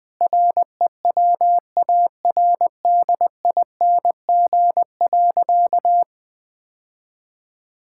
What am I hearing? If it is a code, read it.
Morse code, heard as a keyed tone, about 20 wpm, 700 Hz. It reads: REWARDING.
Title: Sentences with 5-Word Phrases - 20wpm